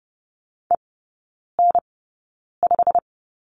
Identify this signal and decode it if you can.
Morse code, heard as a keyed tone, about 30 wpm, spaced as for 9 wpm, 700 Hz. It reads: EN5
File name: Lesson 15e - Number 5 - Three Characters 30wpm